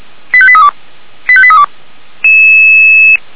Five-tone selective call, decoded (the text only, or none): 85832 85832